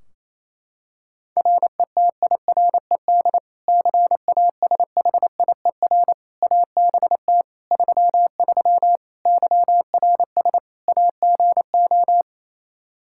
Morse code, heard as a keyed tone, about 28 wpm, 700 Hz. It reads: RETIRED CASHIER ABT 33 YRS AGO
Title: Practice QSO - Exchange 3 - More Info - 28wpm